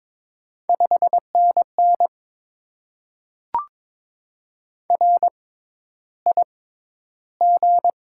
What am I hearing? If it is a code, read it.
Morse code, heard as a keyed tone, about 22 wpm, spaced as for 11 wpm, 700 Hz. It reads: 5NN RIG
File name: Top 20 QSO Elements - 6x Character Spacing - 22wpm